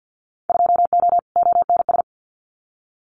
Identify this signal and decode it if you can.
Morse code, heard as a keyed tone, about 50 wpm, 700 Hz. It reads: 2O ONS
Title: Winter Field Day Exchanges Encore 50wpm